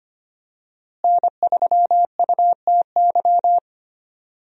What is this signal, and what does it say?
Morse code, heard as a keyed tone, about 25 wpm, 700 Hz. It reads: N3UTY
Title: USA Call Signs - 5 Characters - Repeated 2x - 25wpm